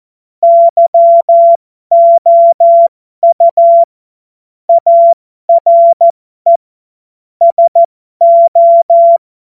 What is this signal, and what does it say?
Morse code, heard as a keyed tone, about 14 wpm, spaced as for 12 wpm, 680 Hz. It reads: YOU ARE SO
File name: Morsekurs / Witze